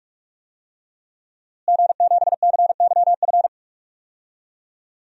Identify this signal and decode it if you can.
Morse code, heard as a keyed tone, about 44 wpm, 700 Hz. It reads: G7CYF
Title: DXCC-100 Call Signs - 5 Characters - Repeated 2x - 45wpm